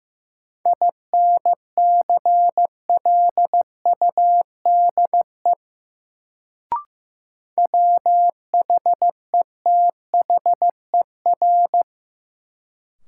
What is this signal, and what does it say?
Morse code, heard as a keyed tone, about 16 wpm, 700 Hz. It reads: INCLUDE WHETHER
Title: Sentences from Top 1000 Words - Review Part 3 - New Word Drill - 15wpm